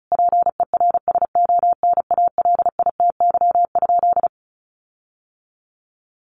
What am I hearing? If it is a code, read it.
Morse code, heard as a keyed tone, about 34 wpm, 700 Hz. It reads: PERSONALITY?
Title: Sentences from Top 1400 Words 35wpm